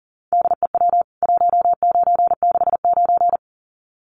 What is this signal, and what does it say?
Morse code, heard as a keyed tone, about 40 wpm, 700 Hz. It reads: DEW 1969
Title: CWT Exchange 40wpm